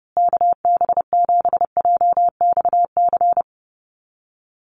Morse code, heard as a keyed tone, about 28 wpm, 700 Hz. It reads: KB7JXC
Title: USA Call Signs Repeated 3x 30wpm